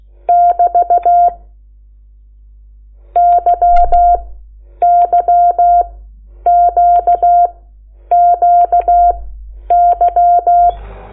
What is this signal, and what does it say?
Morse code, heard as a keyed tone, about 16 wpm, spaced as for 9 wpm, 700 Hz. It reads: <BT> YYQQY
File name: Lecciones / CLASE Nº8